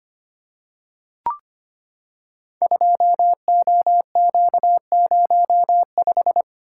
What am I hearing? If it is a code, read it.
Morse code, heard as a keyed tone, about 25 wpm, 700 Hz. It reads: 2OQ05
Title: Code Groups - 5 Letter:Numbers - 25wpm